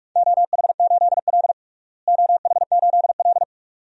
Morse code, heard as a keyed tone, about 42 wpm, 700 Hz. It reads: OH8L OH8L
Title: DXCC-100 Call Signs - 4 Characters - Repeated 2x - 45wpm